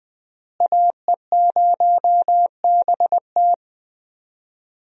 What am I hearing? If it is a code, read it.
Morse code, heard as a keyed tone, about 21 wpm, 700 Hz. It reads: AE0BT
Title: USA Call Signs - 5 Characters - Repeated 2x - 20wpm